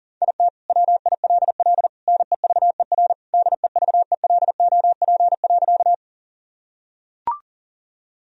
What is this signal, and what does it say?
Morse code, heard as a keyed tone, about 40 wpm, 700 Hz. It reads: IT WILL NEVER DEVELOP.
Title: Sentences from Top 1000 Words - Review Part 2 - 40wpm